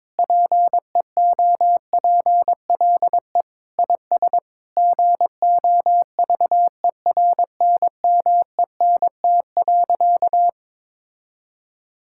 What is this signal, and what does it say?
Morse code, heard as a keyed tone, about 22 wpm, 700 Hz. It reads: PEOPLE IS GOVERNMENT.
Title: Sentences from Top 1000 Words - Review Part 6 - 22wpm